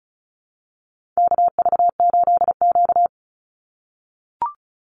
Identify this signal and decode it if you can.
Morse code, heard as a keyed tone, about 34 wpm, 700 Hz. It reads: KV8Q
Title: Call Signs - 35wpm